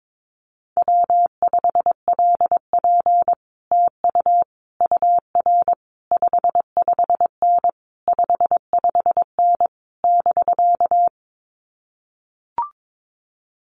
Morse code, heard as a keyed tone, about 22 wpm, 700 Hz. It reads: W5LP TU UR 55N 55N <BK>
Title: Common QSO Phrases Encore 22wpm